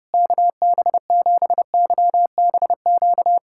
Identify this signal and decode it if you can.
Morse code, heard as a keyed tone, about 30 wpm, 700 Hz. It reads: KB7YBQ
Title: USA Callsign DE Callsign - 6 Characters - 30wpm